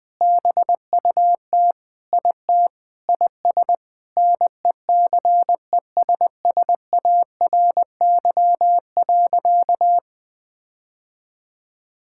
Morse code, heard as a keyed tone, about 21 wpm, 700 Hz. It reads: BUT IT IS NECESSARY.
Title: Sentences with 9-Word Phrases - 20wpm